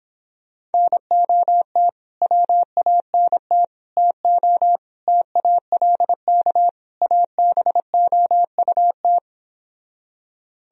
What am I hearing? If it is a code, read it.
Morse code, heard as a keyed tone, about 29 wpm, 700 Hz. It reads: NOT WANT TO TALK ABOUT